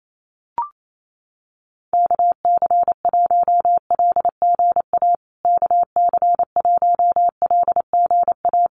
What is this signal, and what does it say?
Morse code, heard as a keyed tone, about 28 wpm, 700 Hz. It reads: KC1LGA KC1LGA
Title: USA Call Signs Repeated 3x 28wpm